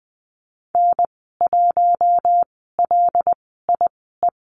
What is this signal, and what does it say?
Morse code, heard as a keyed tone, about 20 wpm, spaced as for 15 wpm, 700 Hz. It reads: N1LIE
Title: 076 Lesson 13g - Letter L - Call Signs 20wpm